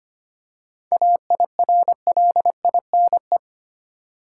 Morse code, heard as a keyed tone, about 25 wpm, 700 Hz. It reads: AIRLINE
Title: Seven Letter Words - 25wpm